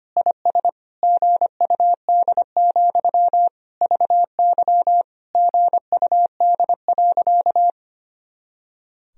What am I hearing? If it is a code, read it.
Morse code, heard as a keyed tone, about 25 wpm, 700 Hz. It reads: IS GUD, VY GUD.
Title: Sentences from Top 200 Words and Abbreviations 25wpm